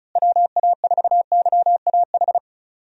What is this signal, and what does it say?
Morse code, heard as a keyed tone, about 36 wpm, 700 Hz. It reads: WA4YAH
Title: USA Call Signs - 6 Characters - Repeated 2x - 35wpm